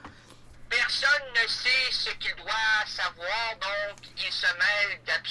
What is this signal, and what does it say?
Emergency Alert System (SAME) tones (header bursts), heard as harsh buzzing bursts, with speech. It is heard from 0.7 to 5.0 s.